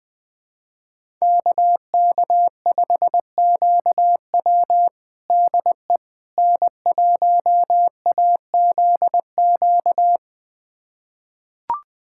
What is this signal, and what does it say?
Morse code, heard as a keyed tone, about 20 wpm, 700 Hz. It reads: KK5QW DE N1AZQ
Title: USA Callsign DE Callsign - 5 Characters - 20wpm